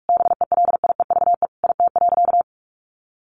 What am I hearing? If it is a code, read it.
Morse code, heard as a keyed tone, about 44 wpm, 700 Hz. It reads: BELIEVE IT.
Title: Sentences from Top 1000 Words 45wpm